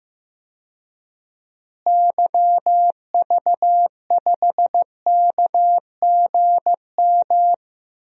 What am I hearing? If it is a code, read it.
Morse code, heard as a keyed tone, about 15 wpm, 700 Hz. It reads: YV5KGM
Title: DXCC-100 Call Signs - 6 Characters - Repeated 3x - 15wpm